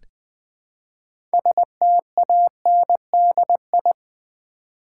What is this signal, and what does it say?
Morse code, heard as a keyed tone, about 20 wpm, 700 Hz. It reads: STANDI
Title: Build A Word - 2 Letters At A Time - 8-Letter Words - 20wpm